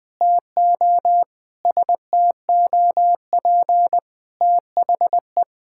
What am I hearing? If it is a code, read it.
Morse code, heard as a keyed tone, about 19 wpm, 700 Hz. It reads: TO STOP THE
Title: Sentences from Top 1000 Words - Review Part 6 - 20wpm